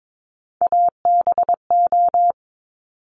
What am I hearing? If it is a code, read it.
Morse code, heard as a keyed tone, about 22 wpm, 700 Hz. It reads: ABO
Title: Top Three Letter Combinations in Words 22wpm